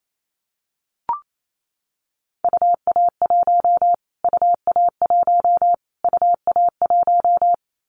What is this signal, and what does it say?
Morse code, heard as a keyed tone, about 28 wpm, 700 Hz. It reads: UA1 UA1 UA1
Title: Callsign Prefixes for Top 25 DXCC - Repeated 3x - 28wpm